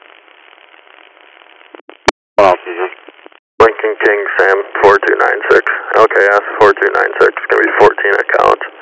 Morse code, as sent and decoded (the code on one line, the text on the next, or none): none